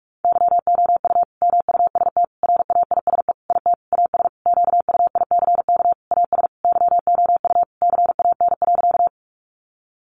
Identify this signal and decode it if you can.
Morse code, heard as a keyed tone, about 44 wpm, 700 Hz. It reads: YOU MUST RAISE IT AS QUICK AS YOU CAN.
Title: Sentences from Top 1400 Words 45wpm